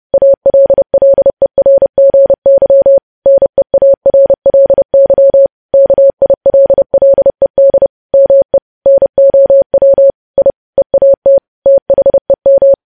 Morse code, heard as a keyed tone, about 30 wpm, 550 Hz. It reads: ALLERGY NEARLY KILLED ME NOW I EAT THEM
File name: podcast / News headlines-1x-30x30